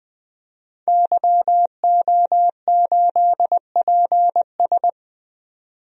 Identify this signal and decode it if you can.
Morse code, heard as a keyed tone, about 21 wpm, 700 Hz. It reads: YO8PS
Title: DXCC-100 Call Signs - 5 Characters - 20wpm